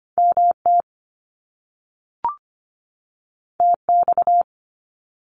Morse code, heard as a keyed tone, about 25 wpm, 700 Hz. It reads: MT TX
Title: US States 25wpm